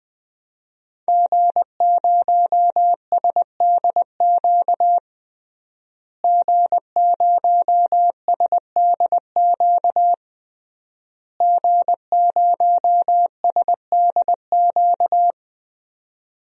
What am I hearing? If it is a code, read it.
Morse code, heard as a keyed tone, about 20 wpm, 700 Hz. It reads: G0SDQ G0SDQ G0SDQ
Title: DXCC-100 Call Signs - 5 Characters - Repeated 3x - 20wpm